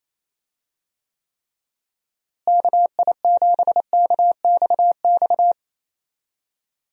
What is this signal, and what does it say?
Morse code, heard as a keyed tone, about 28 wpm, 700 Hz. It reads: KI7KXX
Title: USA Call Signs - 6 Characters - Repeated 3x - 28wpm